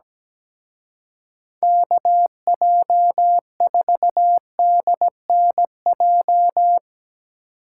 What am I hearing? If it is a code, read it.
Morse code, heard as a keyed tone, about 19 wpm, 700 Hz. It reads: KJ4DNJ
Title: Speed Racing - USA 2x3 Call Signs 17wpm